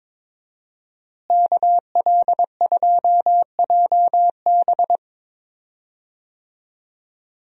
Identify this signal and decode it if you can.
Morse code, heard as a keyed tone, about 22 wpm, 700 Hz. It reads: KL2JB
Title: USA Call Signs - 5 Characters - Repeated 3x - 22wpm